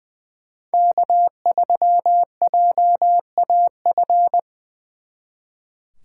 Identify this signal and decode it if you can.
Morse code, heard as a keyed tone, about 24 wpm, 700 Hz. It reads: K3JAF
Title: Speed Racing - USA Call Signs 20wpm